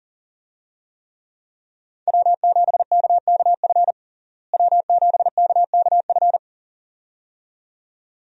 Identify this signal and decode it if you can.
Morse code, heard as a keyed tone, about 40 wpm, 700 Hz. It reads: W7KKF W7KKF
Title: USA Call Signs - 5 Characters - Repeated 2x - 40wpm